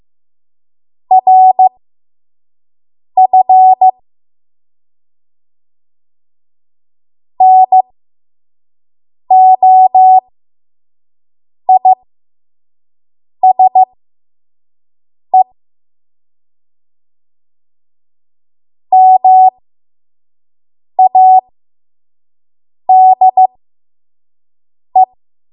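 Morse code, heard as a keyed tone, about 15 wpm, spaced as for 5 wpm, 750 Hz. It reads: RF NOISE MADE
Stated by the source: 5 WPM Morse code practice.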